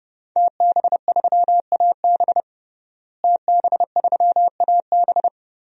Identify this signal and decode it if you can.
Morse code, heard as a keyed tone, about 30 wpm, 700 Hz. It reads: TB3AB TB3AB
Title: DXCC-100 Call Signs - 5 Characters - Repeated 3x - 30wpm